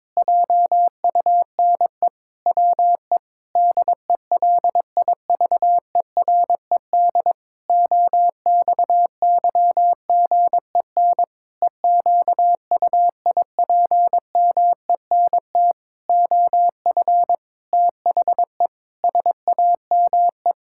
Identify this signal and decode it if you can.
Morse code, heard as a keyed tone, about 22 wpm, 700 Hz. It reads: JUNE WE DELIVERED OXYGEN EQUIPMENT OF THE SAME
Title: Pangram Challenge v2 - 22wpm